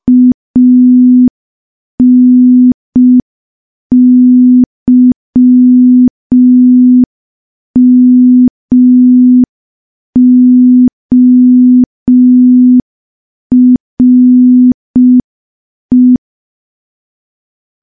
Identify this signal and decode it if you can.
Morse code, heard as a keyed tone, about 5 wpm, 250 Hz. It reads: ANYMORE
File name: morse / morse-training-05WPM